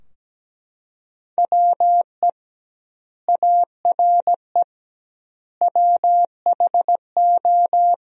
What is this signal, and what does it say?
Morse code, heard as a keyed tone, about 17 wpm, 700 Hz. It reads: WE ARE WHO
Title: 2x Word-Spacing - Sentences from Top 200 Words - 17wpm